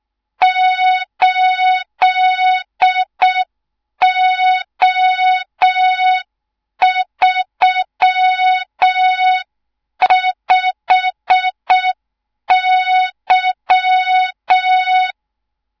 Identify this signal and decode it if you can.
Morse code, heard as a keyed tone, about 6 wpm, 3700 Hz. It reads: 8O35Y
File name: Morze-HA2MN / hang / 24 GYAKORLAT